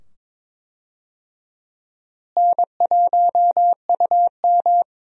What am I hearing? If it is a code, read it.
Morse code, heard as a keyed tone, about 22 wpm, 700 Hz. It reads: N1UM
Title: USA Call Signs - 4 Characters - Repeated 2x - 22wpm